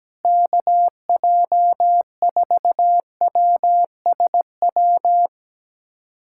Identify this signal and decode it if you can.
Morse code, heard as a keyed tone, about 17 wpm, 700 Hz. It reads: KJ4WSW
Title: USA Callsign DE Callsign - 6 Characters - 17wpm